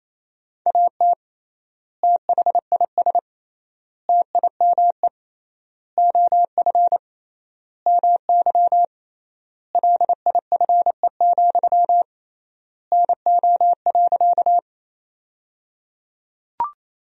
Morse code, heard as a keyed tone, about 28 wpm, 700 Hz. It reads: AT THIS TIME OF MY LIFE, NO.
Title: 3x Word-Spacing - Sentences from Top 300 Words - 28wpm